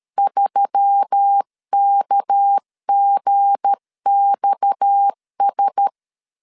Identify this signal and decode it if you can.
Morse code, heard as a keyed tone, about 13 wpm, 800 Hz. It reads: 3KGXS
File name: LEZIONI / 7 GF9Z / 7 G 12wpm 2055